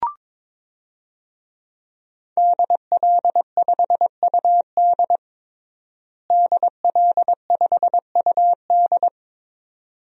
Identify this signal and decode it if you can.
Morse code, heard as a keyed tone, about 22 wpm, 700 Hz. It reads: DL5UD DL5UD
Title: DXCC-100 Call Signs - 5 Characters - Repeated 2x - 22wpm